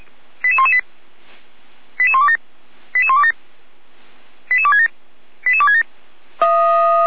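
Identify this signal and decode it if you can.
Five-tone selective call (ZVEI), heard as quick runs of five tones, tones of about 70 ms. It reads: 88118 88127 88127 88267 88267
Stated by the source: ZVEI five-tone call received by SDR.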